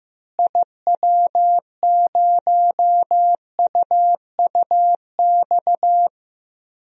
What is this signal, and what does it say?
Morse code, heard as a keyed tone, about 15 wpm, 700 Hz. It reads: IW0UUX
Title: DXCC-100 Call Signs - 6 Characters - Repeated 2x - 15wpm